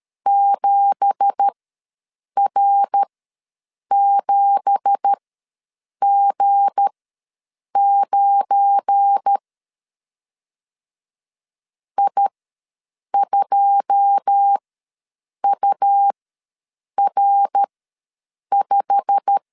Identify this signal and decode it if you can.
Morse code, heard as a keyed tone, about 13 wpm, spaced as for 7 wpm, 800 Hz. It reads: 7R7G9 I2UR5
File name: LEZIONI / 9 2867 ws 12wpm 3431